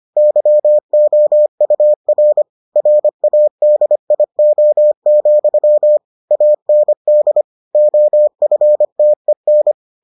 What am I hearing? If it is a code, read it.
Morse code, heard as a keyed tone, about 25 wpm, 600 Hz. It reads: YOUR RADIO, AND OFTEN